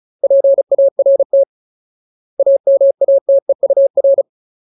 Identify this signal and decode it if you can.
Morse code, heard as a keyed tone, about 37 wpm, 550 Hz. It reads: PART AMATEUR
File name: Ultimate Word List / 35WPM Ultimate